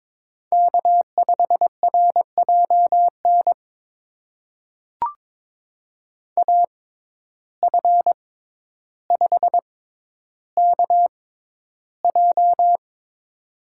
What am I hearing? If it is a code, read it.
Morse code, heard as a keyed tone, about 22 wpm, spaced as for 11 wpm, 700 Hz. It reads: K5RJN AF5KJ
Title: USA Call Signs - 5 Characters - 6x Character Spacing - 22wpm